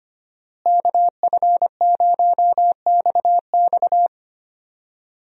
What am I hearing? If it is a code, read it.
Morse code, heard as a keyed tone, about 25 wpm, 700 Hz. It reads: KF0XX
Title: USA Call Signs - 5 Characters - Repeated 2x - 25wpm